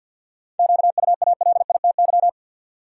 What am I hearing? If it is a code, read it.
Morse code, heard as a keyed tone, about 53 wpm, 700 Hz. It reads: QUALITY
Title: Sentences from Top 1500 Words - New Word Drill Alt - 50wpm